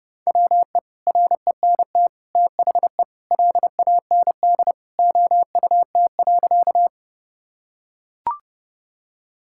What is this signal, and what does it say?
Morse code, heard as a keyed tone, about 32 wpm, 700 Hz. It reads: WE RENT THE LAND OUT.
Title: Sentences from Top 1000 Words - Review Part 5 - 30wpm